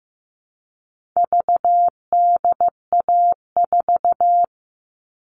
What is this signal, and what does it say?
Morse code, heard as a keyed tone, about 15 wpm, 700 Hz. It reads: VDA4
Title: All Cut Numbers - 4-Digit 15wpm